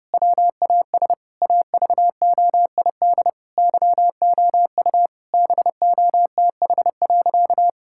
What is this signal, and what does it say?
Morse code, heard as a keyed tone, about 30 wpm, 700 Hz. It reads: WAS AVOID YOU BOTH.
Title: Sentences from Top 1000 Words - Review Part 6 - 30wpm